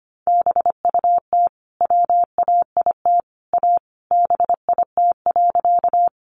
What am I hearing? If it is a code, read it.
Morse code, heard as a keyed tone, about 25 wpm, 700 Hz. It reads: BUT WAIT A BIT.
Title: Sentences from Top 900 Words 25wpm